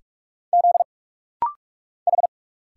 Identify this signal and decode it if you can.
Morse code, heard as a keyed tone, about 46 wpm, 700 Hz. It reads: Z H
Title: Single Letters - Rapid-Fire 45wpm